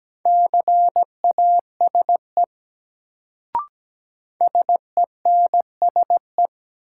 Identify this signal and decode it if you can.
Morse code, heard as a keyed tone, about 17 wpm, 700 Hz. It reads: CASE SENSE
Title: Sentences from Top 700 Words - New Word Drill - 17wpm